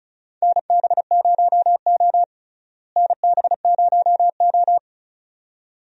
Morse code, heard as a keyed tone, about 36 wpm, 700 Hz. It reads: NB0O NB0O
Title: USA Call Signs - 4 Characters - Repeated 2x - 35wpm